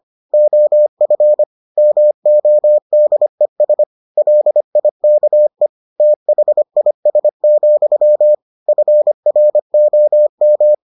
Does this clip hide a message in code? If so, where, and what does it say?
Yes: Morse code from 0.3 to 10.7 s, heard as a keyed tone, about 25 wpm, 600 Hz. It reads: OF MODES LIKE THIS, FROM